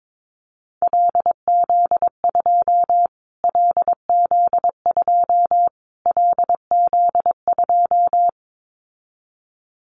Morse code, heard as a keyed tone, about 22 wpm, 700 Hz. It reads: LZ2 LZ2 LZ2
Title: Callsign Prefixes for Top 25 DXCC - Repeated 3x - 22wpm